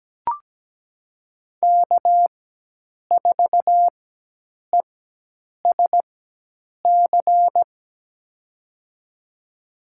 Morse code, heard as a keyed tone, about 17 wpm, spaced as for 8 wpm, 700 Hz. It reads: K4ESC
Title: USA Call Signs - 5 Characters - 4x Character Spacing - 17wpm